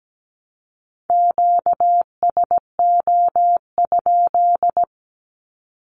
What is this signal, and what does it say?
Morse code, heard as a keyed tone, about 17 wpm, 700 Hz. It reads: QSO?
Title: Q-Codes 17wpm